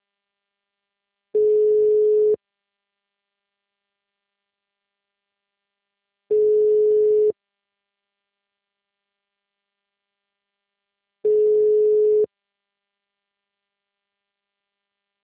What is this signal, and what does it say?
Ringback tone (Europe, 425 Hz).
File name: out- -2030- - - .12